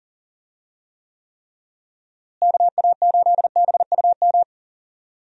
Morse code, heard as a keyed tone, about 39 wpm, 700 Hz. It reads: KA8BUM
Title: USA Call Signs - 6 Characters - Repeated 3x - 40wpm